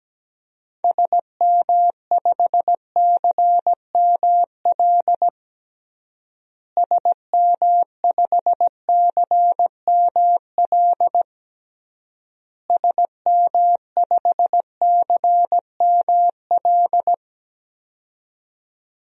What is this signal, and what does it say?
Morse code, heard as a keyed tone, about 17 wpm, 700 Hz. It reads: SM5CML SM5CML SM5CML
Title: DXCC-100 Call Signs - 6 Characters - Repeated 3x - 17wpm